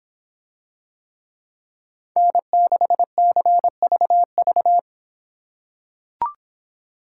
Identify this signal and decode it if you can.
Morse code, heard as a keyed tone, about 26 wpm, 700 Hz. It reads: N6CVV